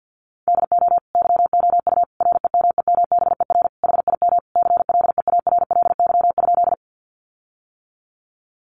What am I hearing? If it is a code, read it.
Morse code, heard as a keyed tone, about 48 wpm, 700 Hz. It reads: DO YOU REMEMBER HIM CLEARLY?
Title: Sentences from Top 1300 Words 50wpm